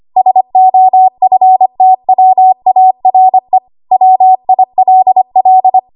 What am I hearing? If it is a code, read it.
Morse code, heard as a keyed tone, about 25 wpm, 750 Hz. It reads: SOFTWARE WILL
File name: Archive / 25 WPM / 25WPM